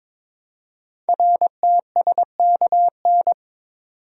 Morse code, heard as a keyed tone, about 23 wpm, 700 Hz. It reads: RTSKN
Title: Code Groups - 5 Letters - 22wpm